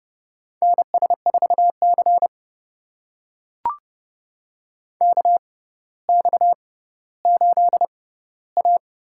Morse code, heard as a keyed tone, about 30 wpm, spaced as for 15 wpm, 700 Hz. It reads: NS4C KX8A
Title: USA Call Signs - 4 Characters - 6x Character Spacing - 30wpm